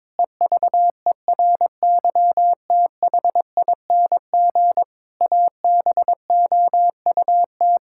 Morse code, heard as a keyed tone, about 22 wpm, 700 Hz. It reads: EVERYTHING ABOUT
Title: Pangram Challenge v2 - 22wpm